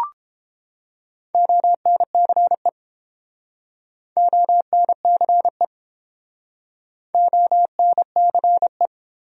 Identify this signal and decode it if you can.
Morse code, heard as a keyed tone, about 30 wpm, 700 Hz. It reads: ONCE ONCE ONCE